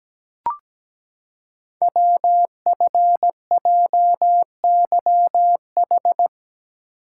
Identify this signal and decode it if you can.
Morse code, heard as a keyed tone, about 17 wpm, 700 Hz. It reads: WFJYH
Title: Code Groups - 5 Letters - 17wpm